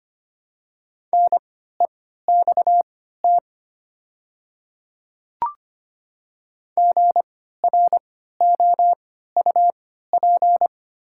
Morse code, heard as a keyed tone, about 25 wpm, spaced as for 14 wpm, 700 Hz. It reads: NEXT GROUP
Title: 3x Character Spacing - Top 300 Words - 25wpm